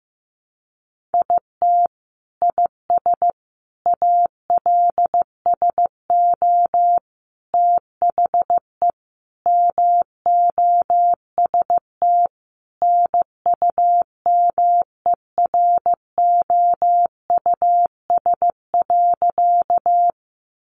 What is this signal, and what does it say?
Morse code, heard as a keyed tone, about 15 wpm, 700 Hz. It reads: IT IS ALSO THE MOST NUMEROUS.
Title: Sentences from Top 1400 Words 15wpm